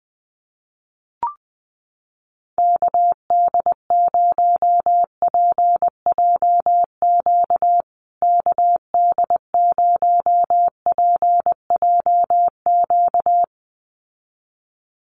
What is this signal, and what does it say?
Morse code, heard as a keyed tone, about 20 wpm, 700 Hz. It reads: KD0PJQ KD0PJQ
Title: USA Call Signs Repeated 2x 20wpm